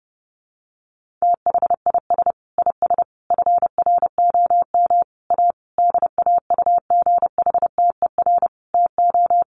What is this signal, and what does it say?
Morse code, heard as a keyed tone, about 30 wpm, 700 Hz. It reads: THIS IS FROM A DAUGHTER TO
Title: Sentences from Top 1500 Words 30wpm